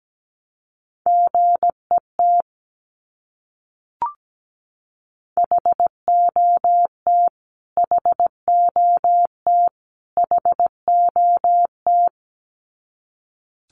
Morse code, heard as a keyed tone, about 17 wpm, 700 Hz. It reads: GET HOT HOT HOT
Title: Top 200 Words Repeated 17wpm